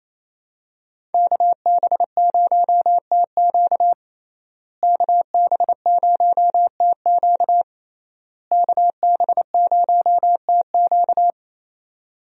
Morse code, heard as a keyed tone, about 28 wpm, 700 Hz. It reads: KB0TQ KB0TQ KB0TQ
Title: USA Call Signs - 5 Characters - Repeated 3x - 28wpm